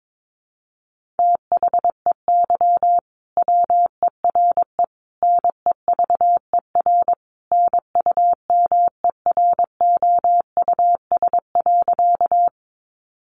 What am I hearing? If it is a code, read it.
Morse code, heard as a keyed tone, about 22 wpm, 700 Hz. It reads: THEY WERE NEVER NUMEROUS.
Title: Sentences from Top 1400 Words 22wpm